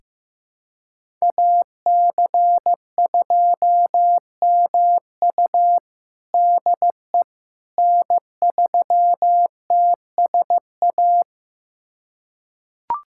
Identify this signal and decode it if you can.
Morse code, heard as a keyed tone, about 15 wpm, 700 Hz. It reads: AC2MU DE N3TSA
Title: USA Callsign DE Callsign - 5 Characters - 15wpm